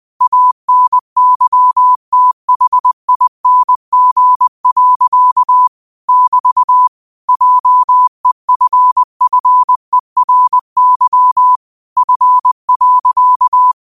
Morse code, heard as a keyed tone, about 20 wpm, 1000 Hz. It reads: ANYTHING. <BT> JEFFERY F.
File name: qotdfiles / 20wpm